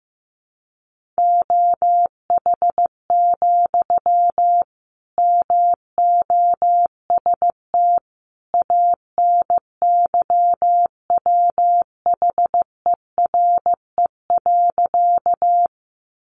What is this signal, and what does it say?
Morse code, heard as a keyed tone, about 14 wpm, 700 Hz. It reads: OH, MOST ANYWHERE.
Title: Sentences from Top 1400 Words 15wpm